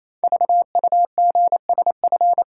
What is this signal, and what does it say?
Morse code, heard as a keyed tone, about 29 wpm, 700 Hz. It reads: VUGSF
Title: Code Groups - 5 Letters - 28wpm